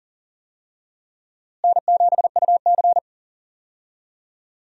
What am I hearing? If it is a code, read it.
Morse code, heard as a keyed tone, about 40 wpm, 700 Hz. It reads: N7UC
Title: USA Call Signs - 4 Characters - Repeated 2x - 40wpm